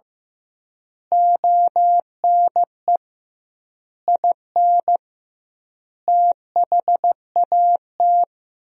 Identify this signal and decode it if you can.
Morse code, heard as a keyed tone, about 15 wpm, 700 Hz. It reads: ONE IN THAT
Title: 2x Word-Spacing - Sentences from Top 500 Words - 15wpm